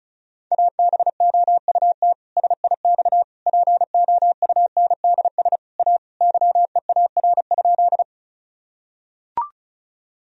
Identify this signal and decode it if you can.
Morse code, heard as a keyed tone, about 36 wpm, 700 Hz. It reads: ABOUT SIX POUNDS A YEAR?
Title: Sentences from Top 1000 Words - Review Part 2 - 35wpm